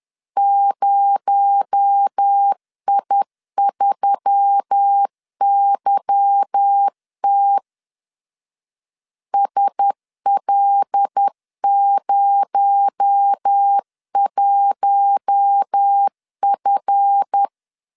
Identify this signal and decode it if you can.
Morse code, heard as a keyed tone, about 10 wpm, 800 Hz. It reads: 0I3YT SL01F
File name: CW / DET / 1 9 1500C S B 10wpm 4439